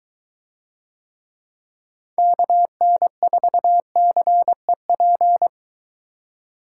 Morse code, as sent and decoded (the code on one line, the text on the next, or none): -.- -. ....- -.-. .--.
KN4CEP